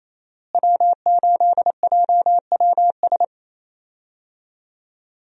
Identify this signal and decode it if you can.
Morse code, heard as a keyed tone, about 28 wpm, 700 Hz. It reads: W8JWS
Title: USA Call Signs - 5 Characters - Repeated 2x - 28wpm